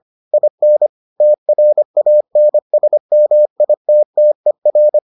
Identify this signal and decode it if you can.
Morse code, heard as a keyed tone, about 25 wpm, 600 Hz. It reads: IN TRANSMITTER